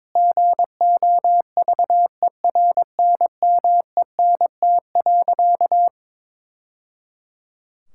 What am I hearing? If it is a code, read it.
Morse code, heard as a keyed tone, about 21 wpm, 700 Hz. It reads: GOVERNMENT.
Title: Sentences from Top 1000 Words - Review Part 6 - 22wpm